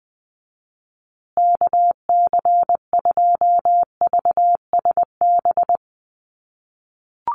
Morse code, heard as a keyed tone, about 20 wpm, 700 Hz. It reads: KC2VSB
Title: USA Call Signs Repeated 3x 20wpm